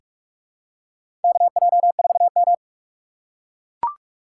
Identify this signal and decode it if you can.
Morse code, heard as a keyed tone, about 43 wpm, 700 Hz. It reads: KJ4M
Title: USA Call Signs - 4 Characters - 2x Character Spacing - 45wpm